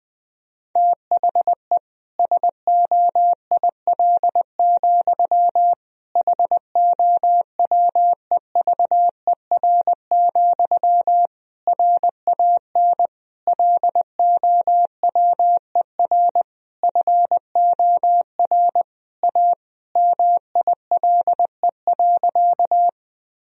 Morse code, heard as a keyed tone, about 21 wpm, 700 Hz. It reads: THE SOIL, HOWEVER, RAN LOWER FOR A MILE.
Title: Sentences from Top 1000 Words - Review Part 4 - 20wpm